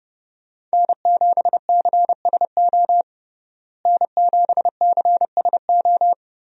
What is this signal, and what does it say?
Morse code, heard as a keyed tone, about 32 wpm, 700 Hz. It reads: N7CSO N7CSO
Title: USA Call Signs - 5 Characters - Repeated 3x - 30wpm